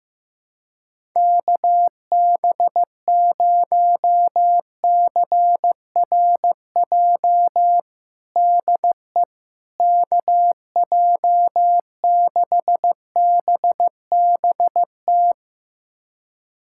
Morse code, heard as a keyed tone, about 15 wpm, 700 Hz. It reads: KB0CRJ DE KJ6BBT
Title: USA Callsign DE Callsign - 6 Characters - 15wpm